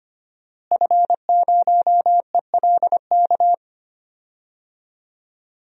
Morse code, heard as a keyed tone, about 25 wpm, 700 Hz. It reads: F0ELK
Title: DXCC-100 Call Signs - 5 Characters - Repeated 2x - 25wpm